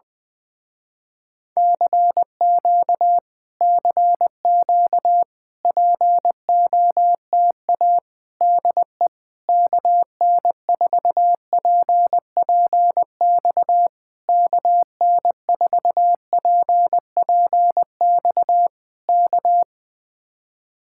Morse code, heard as a keyed tone, about 20 wpm, 700 Hz. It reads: CQ CQ POTA DE KN4PPX KN4PPX K
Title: Speed-Racing Common QSO Phrases - 17wpm